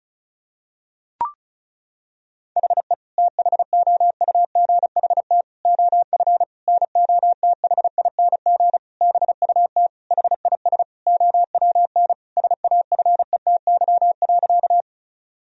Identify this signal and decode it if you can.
Morse code, heard as a keyed tone, about 34 wpm, 700 Hz. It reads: HE THOUGHT OF NOTHING BUT HIS OWN SAFETY.
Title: Sentences from Top 1000 Words - Review Part 4 - 35wpm